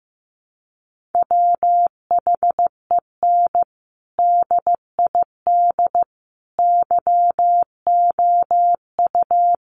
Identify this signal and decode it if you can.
Morse code, heard as a keyed tone, about 15 wpm, 700 Hz. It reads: WHEN DID YOU
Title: Sentences from Top 1400 Words 15wpm